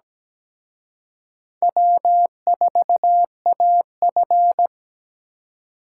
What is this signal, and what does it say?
Morse code, heard as a keyed tone, about 20 wpm, 700 Hz. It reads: W4AF
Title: Speed Racing - USA 1x2 Call Signs 17wpm